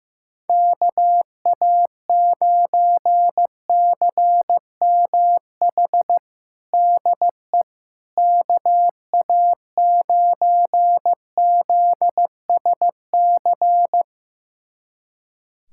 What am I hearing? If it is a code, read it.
Morse code, heard as a keyed tone, about 15 wpm, 700 Hz. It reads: KA9CMH DE KA9ZSC
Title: USA Callsign DE Callsign 15wpm